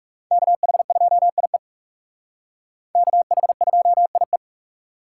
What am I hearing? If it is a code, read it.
Morse code, heard as a keyed tone, about 36 wpm, 700 Hz. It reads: KH2IE KH2IE
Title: Speed Racing - USA 2x2 Call Signs 30wpm